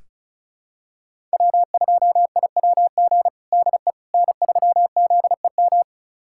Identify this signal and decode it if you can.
Morse code, heard as a keyed tone, about 36 wpm, 700 Hz. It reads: W2IWG DE N3ZEM
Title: USA Callsign DE Callsign - 5 Characters - 35wpm